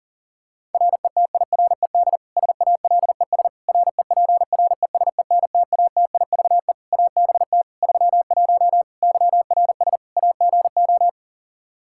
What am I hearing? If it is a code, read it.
Morse code, heard as a keyed tone, about 37 wpm, 700 Hz. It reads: RETIRED SALES REPRESENTATIVE ABT 31 YRS AGO
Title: Practice QSO - Exchange 3 - More Info - 40wpm